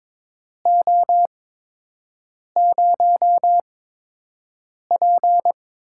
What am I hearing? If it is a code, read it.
Morse code, heard as a keyed tone, about 22 wpm, spaced as for 6 wpm, 700 Hz. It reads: O0P
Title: USA Call Signs - 4 Characters - 8x Character Spacing - 22wpm